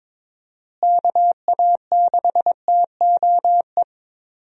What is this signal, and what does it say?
Morse code, heard as a keyed tone, about 22 wpm, 700 Hz. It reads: KA6TOE
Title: USA Call Signs - 6 Characters - Repeated 2x - 22wpm